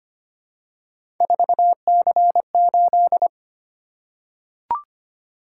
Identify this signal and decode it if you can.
Morse code, heard as a keyed tone, about 25 wpm, 700 Hz. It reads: 4C8
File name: Three Letter-Number Combinations - 25wpm